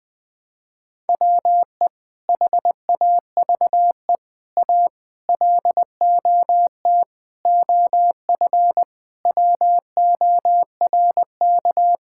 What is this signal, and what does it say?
Morse code, heard as a keyed tone, about 21 wpm, 700 Hz. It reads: WE HAVE A LOT OF WORK
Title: Sentences with 7-Word Phrases - Spoken Beginning - 20wpm